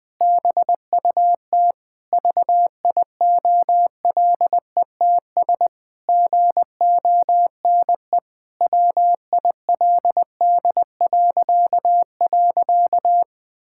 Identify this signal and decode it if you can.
Morse code, heard as a keyed tone, about 21 wpm, 700 Hz. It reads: BUT VIOLETS GONE WILD..
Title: Pangram Challenge v2 - 20wpm